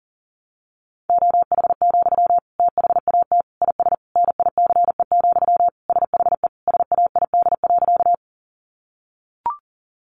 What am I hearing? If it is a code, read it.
Morse code, heard as a keyed tone, about 40 wpm, 700 Hz. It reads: OH, THAT IS NICE, SHE SAID.
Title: Sentences from Top 1100 Words 40wpm